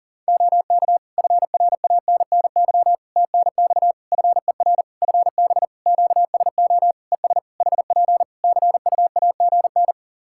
Morse code, heard as a keyed tone, about 40 wpm, 700 Hz. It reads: OK FRANNY TNX FER FB QSO ES HP CUAGN
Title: Practice QSO - Exchange 4 - Ending - 40wpm